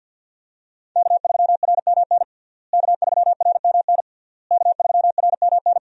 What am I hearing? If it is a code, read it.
Morse code, heard as a keyed tone, about 47 wpm, 700 Hz. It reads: K3RMN K3RMN K3RMN
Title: USA Call Signs - 5 Characters - Repeated 3x - 50wpm